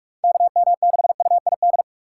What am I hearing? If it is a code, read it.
Morse code, heard as a keyed tone, about 44 wpm, 700 Hz. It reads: KM6UID